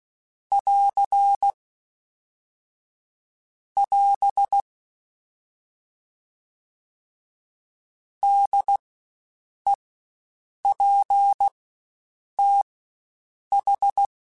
Morse code, heard as a keyed tone, about 16 wpm, spaced as for 7 wpm, 800 Hz. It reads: <AR> <AS> DEPTH
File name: Spaced Alphas / Spaced Alphas mp3s / Ds sp @15